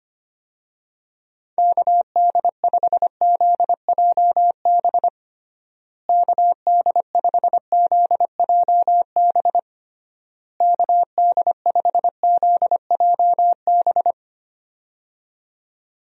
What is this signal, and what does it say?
Morse code, heard as a keyed tone, about 25 wpm, 700 Hz. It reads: KD5ZJB KD5ZJB KD5ZJB
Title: USA Call Signs - 6 Characters - Repeated 3x - 25wpm